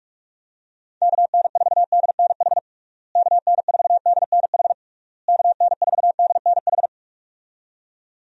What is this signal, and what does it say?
Morse code, heard as a keyed tone, about 47 wpm, 700 Hz. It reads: KN4DNH KN4DNH KN4DNH
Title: USA Call Signs - 6 Characters - Repeated 3x - 45wpm